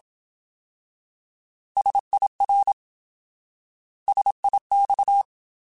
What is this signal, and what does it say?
Morse code, heard as a keyed tone, about 27 wpm, 800 Hz. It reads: SIR SIX
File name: Alphabetical / Alphabetical mp3s / Ss @25